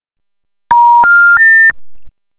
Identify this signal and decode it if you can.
Special information tone.